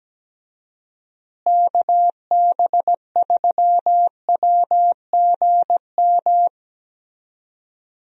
Morse code, heard as a keyed tone, about 17 wpm, 700 Hz. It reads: KB3WGM
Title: USA Call Signs - 6 Characters - Repeated 3x - 17wpm